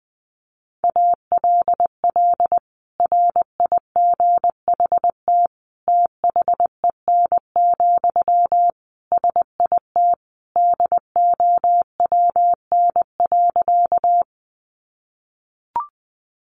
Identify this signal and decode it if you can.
Morse code, heard as a keyed tone, about 20 wpm, 700 Hz. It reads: ALL RIGHT THEN, SIT DOWN.
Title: Sentences from Top 500 Words 20wpm